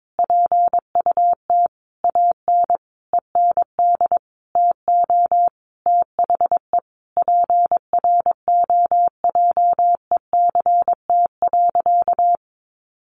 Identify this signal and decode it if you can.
Morse code, heard as a keyed tone, about 22 wpm, 700 Hz. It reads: PUT AN END TO THE PROJECT.
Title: Sentences from Top 1400 Words 22wpm